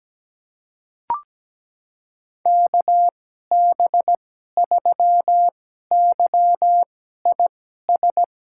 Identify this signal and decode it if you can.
Morse code, heard as a keyed tone, about 17 wpm, spaced as for 12 wpm, 700 Hz. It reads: KB3YIS
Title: USA Call Signs - 6 Characters - 2x Character Spacing - 17wpm